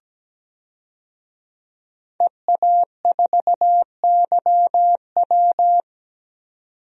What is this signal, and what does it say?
Morse code, heard as a keyed tone, about 17 wpm, 700 Hz. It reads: EA4YW
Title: DXCC-100 Call Signs - 5 Characters - 17wpm